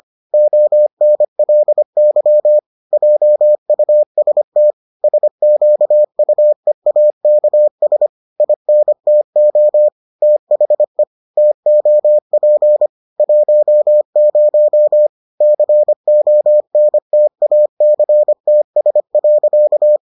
Morse code, heard as a keyed tone, about 25 wpm, 600 Hz. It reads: ONLY JUST SQUEAKS INTO THE TOP 10 CONTACTS.